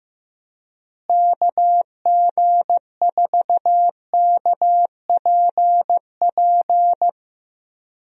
Morse code, heard as a keyed tone, about 15 wpm, 700 Hz. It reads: KG4KPP
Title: USA Call Signs - 6 Characters - Repeated 3x - 15wpm